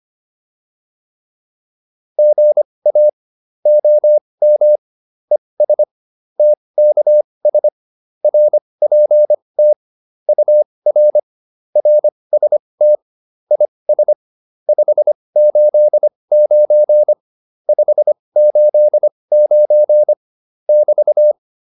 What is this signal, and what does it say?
Morse code, heard as a keyed tone, about 25 wpm, spaced as for 20 wpm, 600 Hz. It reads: GA OM ES TKS RPT UR RST IS 589 589 <BT>